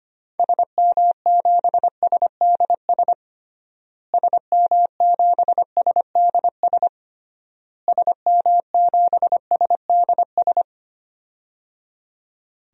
Morse code, heard as a keyed tone, about 25 wpm, 700 Hz. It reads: SM7SDS SM7SDS SM7SDS
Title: DXCC-100 Call Signs - 6 Characters - Repeated 3x - 25wpm